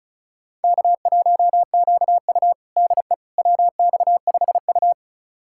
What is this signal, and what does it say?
Morse code, heard as a keyed tone, about 36 wpm, 700 Hz. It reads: K1QU DE WX5U
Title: USA Callsign DE Callsign - 4 Characters - 35wpm